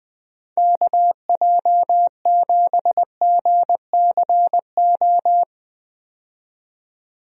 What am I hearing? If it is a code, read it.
Morse code, heard as a keyed tone, about 20 wpm, 700 Hz. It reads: KJ7GCO
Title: USA Call Signs - 6 Characters - 20wpm